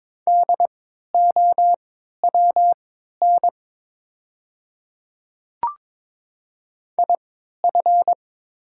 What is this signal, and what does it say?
Morse code, heard as a keyed tone, about 22 wpm, spaced as for 12 wpm, 700 Hz. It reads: DOWN IF
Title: 3x Character Spacing - Top 100 Words - 22wpm